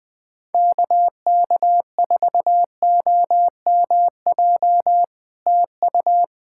Morse code, heard as a keyed tone, about 20 wpm, 700 Hz. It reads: KK4OMJ TU
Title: POTA QSO Practice - 20wpm